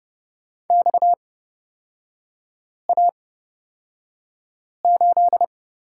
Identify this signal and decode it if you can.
Morse code, heard as a keyed tone, about 29 wpm, spaced as for 5 wpm, 690 Hz. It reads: XA8
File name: Lesson 37e - Letter x - Three Characters 30wpm